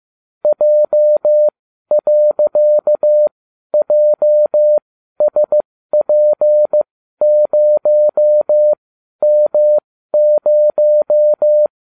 Morse code, heard as a keyed tone, about 15 wpm, spaced as for 12 wpm, 600 Hz. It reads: J.JSP0M0